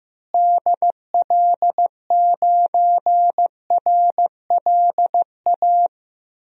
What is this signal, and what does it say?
Morse code, heard as a keyed tone, about 15 wpm, 700 Hz. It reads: DL9RLA
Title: DXCC-100 Call Signs - 6 Characters - Repeated 3x - 15wpm